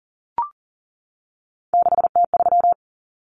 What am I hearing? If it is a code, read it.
Morse code, heard as a keyed tone, about 40 wpm, 700 Hz. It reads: 6T3
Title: Common Cut Numbers - 3-Digit 40wpm